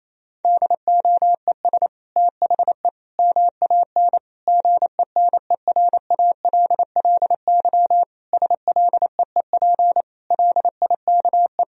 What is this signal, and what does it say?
Morse code, heard as a keyed tone, about 28 wpm, 700 Hz. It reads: DOES THE MAN GENERALLY SLEEP LIKE
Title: Sentences from Top 1000 Words - Review Part 4 - 28wpm